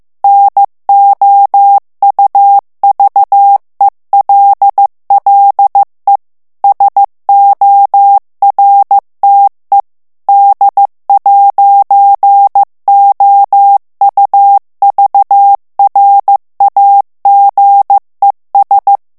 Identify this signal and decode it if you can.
Morse code, heard as a keyed tone, about 15 wpm, 800 Hz. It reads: NOUVELLE SORTE D'OUVRAGES
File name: morse mp3 / FRANCAIS TEXTE CLAIR 2 VITESSE 14 MOTS MINUTES